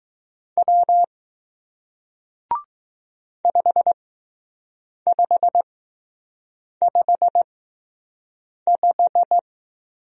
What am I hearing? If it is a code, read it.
Morse code, heard as a keyed tone, about 20 wpm, spaced as for 6 wpm, 700 Hz. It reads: W 5555